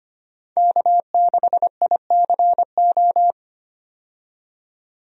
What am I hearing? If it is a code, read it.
Morse code, heard as a keyed tone, about 25 wpm, 700 Hz. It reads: K6ICO
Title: USA Call Signs - 5 Characters - 25wpm